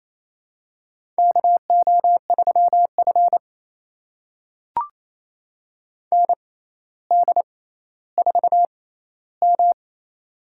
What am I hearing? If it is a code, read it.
Morse code, heard as a keyed tone, about 28 wpm, spaced as for 14 wpm, 700 Hz. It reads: KO3F ND4M
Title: USA Call Signs - 4 Characters - 6x Character Spacing - 28wpm